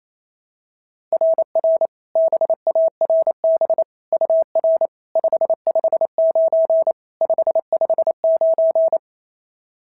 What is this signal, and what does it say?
Morse code, heard as a keyed tone, about 25 wpm, 650 Hz. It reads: RR BARB UR 559 559
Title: SOTA QSO Practice 28wpm